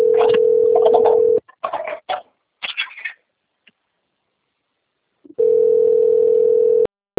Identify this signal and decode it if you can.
Ringback tone (North America), 440 + 480 Hz.